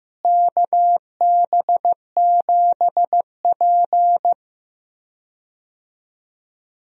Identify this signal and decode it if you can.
Morse code, heard as a keyed tone, about 15 wpm, 700 Hz. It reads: KB7P
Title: USA Call Signs - 4 Characters - Repeated 3x - 15wpm